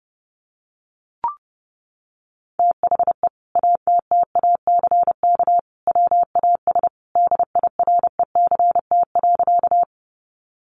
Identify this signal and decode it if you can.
Morse code, heard as a keyed tone, about 30 wpm, 700 Hz. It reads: THE ATTACK WAS DIRECT.
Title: Sentences from Top 1500 Words 30wpm